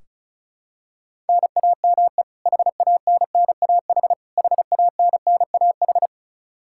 Morse code, heard as a keyed tone, about 36 wpm, 700 Hz. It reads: NAME HANNAH HANNAH
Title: Practice QSO - Exchange 1 - Beginning - 35wpm